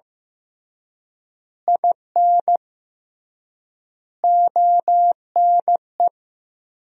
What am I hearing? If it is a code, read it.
Morse code, heard as a keyed tone, about 15 wpm, 700 Hz. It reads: IN ONE
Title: 3x Word-Spacing - Sentences from Top 500 Words - 15wpm